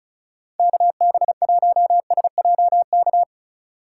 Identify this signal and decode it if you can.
Morse code, heard as a keyed tone, about 34 wpm, 700 Hz. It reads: KB1SJK
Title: USA Call Signs - 6 Characters - 2x Character Spacing - 35wpm